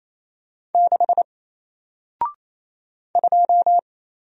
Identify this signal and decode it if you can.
Morse code, heard as a keyed tone, about 28 wpm, 700 Hz. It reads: 6 2
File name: Single-Letter-Number-28wpm